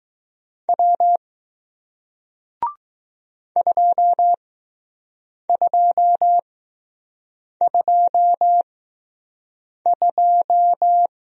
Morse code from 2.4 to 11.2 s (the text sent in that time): 2222